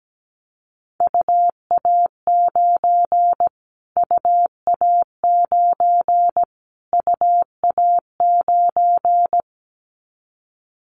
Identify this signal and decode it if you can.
Morse code, heard as a keyed tone, about 17 wpm, 700 Hz. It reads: UA9 UA9 UA9
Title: Callsign Prefixes for Top 25 DXCC - Repeated 3x - 17wpm